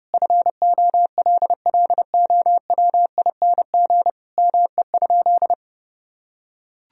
Morse code, heard as a keyed tone, about 30 wpm, 700 Hz. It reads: FOLLOWING ME?
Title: Sentences from Top 1000 Words - Review Part 4 - 30wpm